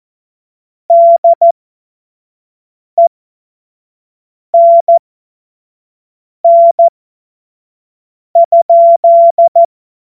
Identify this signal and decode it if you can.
Morse code, heard as a keyed tone, about 14 wpm, spaced as for 5 wpm, 680 Hz. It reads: DENN?
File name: Morsekurs / Witze / Witz 03. Mai 2022 25 F70